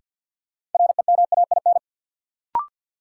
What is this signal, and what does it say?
Morse code, heard as a keyed tone, about 46 wpm, 700 Hz. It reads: REMAIN
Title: Sentences from Top 1200 Words - New Word Drill Alt - 50wpm